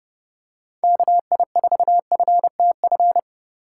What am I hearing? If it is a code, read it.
Morse code, heard as a keyed tone, about 32 wpm, 700 Hz. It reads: KI4FTF
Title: USA Call Signs - 6 Characters - 4x Character Spacing - 30wpm